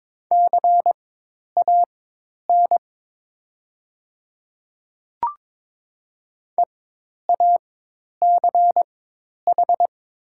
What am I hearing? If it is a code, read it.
Morse code, heard as a keyed tone, about 22 wpm, spaced as for 10 wpm, 700 Hz. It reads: CAN EACH
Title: 4x Character Spacing - Top 500 Words - 22wpm